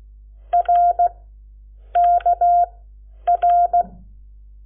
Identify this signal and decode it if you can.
Morse code, heard as a keyed tone, about 16 wpm, spaced as for 8 wpm, 700 Hz. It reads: RKR
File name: Lecciones / CLASE Nº7